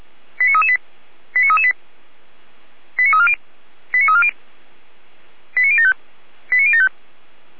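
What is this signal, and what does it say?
Five-tone selective call (ZVEI), heard as quick runs of five tones, tones of about 80 ms. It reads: 89338 89338 89340 89340 89075 89075
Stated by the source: ZVEI five-tone call 89337 received by SDR, from the feuerwehr-orlamuende.org fire brigade server.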